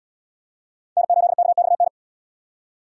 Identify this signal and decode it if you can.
Morse code, heard as a keyed tone, about 70 wpm, 700 Hz. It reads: N8KXN